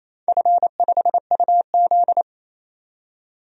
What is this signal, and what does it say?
Morse code, heard as a keyed tone, about 28 wpm, 700 Hz. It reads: F5UZ
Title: DXCC-100 Call Signs - 4 Characters - 28wpm